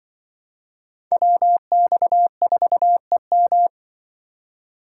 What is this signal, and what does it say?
Morse code, heard as a keyed tone, about 23 wpm, 700 Hz. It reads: WX4EM
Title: Speed Racing - USA 2x2 Call Signs 20wpm